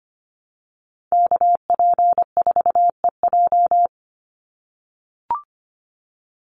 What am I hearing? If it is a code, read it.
Morse code, heard as a keyed tone, about 25 wpm, 700 Hz. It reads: KP4EJ
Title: Call Signs - 25wpm